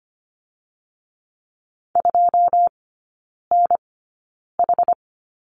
Morse code, heard as a keyed tone, about 25 wpm, spaced as for 9 wpm, 690 Hz. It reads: 2NH
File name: Lesson 14e - Number 2 - Three Characters 25wpm